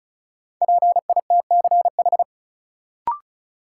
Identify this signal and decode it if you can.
Morse code, heard as a keyed tone, about 34 wpm, 700 Hz. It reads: PITCH
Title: Sentences from Top 1300 Words - New Word Drill Alt - 35wpm